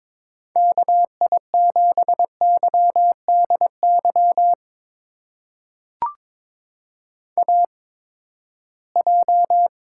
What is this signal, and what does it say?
Morse code, heard as a keyed tone, about 22 wpm, spaced as for 15 wpm, 700 Hz. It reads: KI7YDY AJ
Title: USA Call Signs - 8x Character Spacing - 22wpm